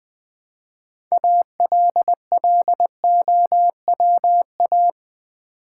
Morse code, heard as a keyed tone, about 20 wpm, 700 Hz. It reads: ALLOWA
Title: Build A Word - 3 Letters At A Time - 9-Letter Words - 20wpm